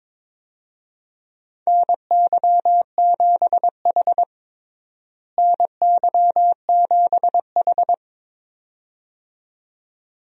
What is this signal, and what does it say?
Morse code, heard as a keyed tone, about 22 wpm, 700 Hz. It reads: NY7H NY7H
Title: USA Call Signs - 4 Characters - Repeated 2x - 22wpm